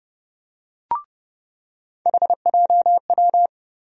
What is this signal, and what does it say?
Morse code, heard as a keyed tone, about 30 wpm, 700 Hz. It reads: HJW
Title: USA Call Sign Postfixes - 3 Letters 30wpm